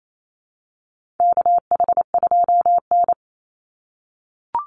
Morse code, heard as a keyed tone, about 28 wpm, 700 Hz. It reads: KH2N
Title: Call Signs - 28wpm